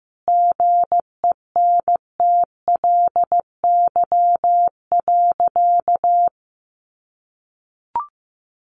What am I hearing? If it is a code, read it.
Morse code, heard as a keyed tone, about 15 wpm, 700 Hz. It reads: GENTLY.
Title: Sentences from Top 1500 Words 15wpm